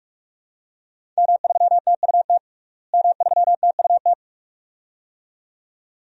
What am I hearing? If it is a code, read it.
Morse code, heard as a keyed tone, about 42 wpm, 700 Hz. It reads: M3TUT M3TUT
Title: DXCC-100 Call Signs - 5 Characters - Repeated 2x - 45wpm